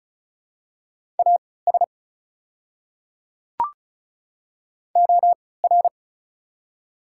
Morse code, heard as a keyed tone, about 34 wpm, spaced as for 20 wpm, 700 Hz. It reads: AS OR
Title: 3x Character Spacing - Top 100 Words - 35wpm